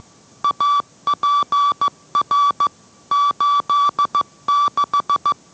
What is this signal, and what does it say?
Morse code, heard as a keyed tone, about 15 wpm, 1160 Hz. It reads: APR 86